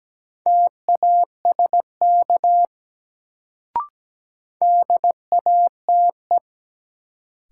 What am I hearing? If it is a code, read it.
Morse code, heard as a keyed tone, about 17 wpm, 700 Hz. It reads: TASK DATE
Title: Sentences from Top 1100 Words - New Word Drill Alt - 17wpm